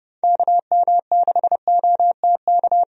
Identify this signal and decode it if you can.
Morse code, heard as a keyed tone, about 30 wpm, 700 Hz. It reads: KM6OTK
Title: USA Call Signs - 6 Characters - 30wpm